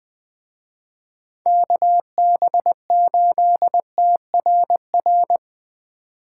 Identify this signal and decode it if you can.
Morse code, heard as a keyed tone, about 21 wpm, 700 Hz. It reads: KB8TRR
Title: Speed Racing - USA 2x3 Call Signs 17wpm